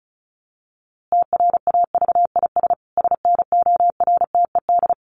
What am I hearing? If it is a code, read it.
Morse code, heard as a keyed tone, about 34 wpm, 700 Hz. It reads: TRAVIS SNORTED
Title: A Sound Of Thunder 35wpm